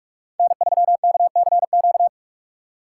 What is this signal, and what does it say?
Morse code, heard as a keyed tone, about 42 wpm, 690 Hz. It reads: N3KCQ
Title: USA Call Signs - 5 Characters - 6x Character Spacing - 45wpm